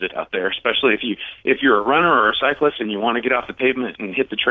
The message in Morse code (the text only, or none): none